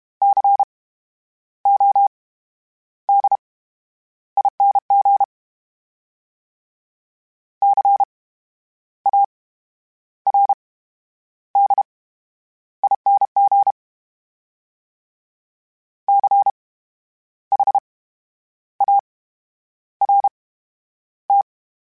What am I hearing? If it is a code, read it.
Morse code, heard as a keyed tone, about 32 wpm, spaced as for 10 wpm, 800 Hz. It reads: CODING CARDING CHART
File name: Special INGs / Special INGs-mp3s / VerySlowToFaster Ing @30